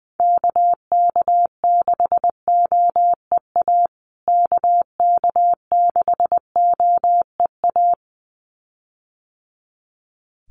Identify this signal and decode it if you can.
Morse code, heard as a keyed tone, about 20 wpm, 700 Hz. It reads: KK6OEA KK6OEA
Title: USA Call Signs Repeated 2x 20wpm